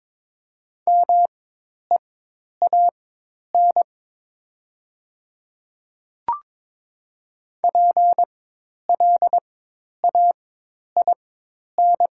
Morse code, heard as a keyed tone, about 22 wpm, spaced as for 10 wpm, 700 Hz. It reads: MEAN PLAIN
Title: 4x Character Spacing - Top 500 Words - 22wpm